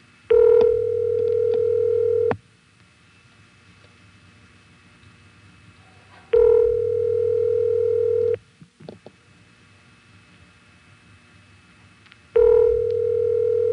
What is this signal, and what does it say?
Ringback tone (North America), 440 + 480 Hz.